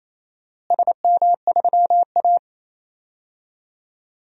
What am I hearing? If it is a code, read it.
Morse code, heard as a keyed tone, about 28 wpm, 700 Hz. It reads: SM3A
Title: DXCC-100 Call Signs - 4 Characters - Repeated 3x - 28wpm